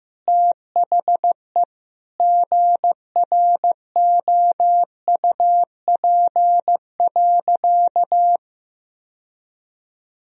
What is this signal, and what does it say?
Morse code, heard as a keyed tone, about 15 wpm, 700 Hz. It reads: THE GROUP.
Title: Sentences from Top 1000 Words - Review Part 1 - 15wpm